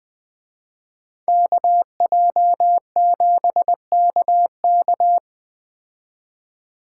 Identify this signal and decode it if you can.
Morse code, heard as a keyed tone, about 21 wpm, 700 Hz. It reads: KJ7KK